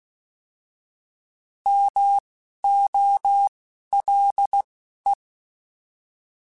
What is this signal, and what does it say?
Morse code, heard as a keyed tone, about 16 wpm, spaced as for 11 wpm, 800 Hz. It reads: MOLE
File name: Alphabetical / Alphabetical mp3s / Ms @10